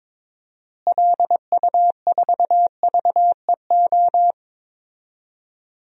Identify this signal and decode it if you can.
Morse code, heard as a keyed tone, about 22 wpm, 700 Hz. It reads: LU4VEO
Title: DXCC-100 Call Signs - 6 Characters - Repeated 2x - 22wpm